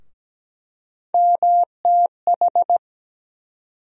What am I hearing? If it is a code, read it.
Morse code, heard as a keyed tone, about 17 wpm, 700 Hz. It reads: MTH
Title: USA Call Sign Postfixes - 3 Letters 17wpm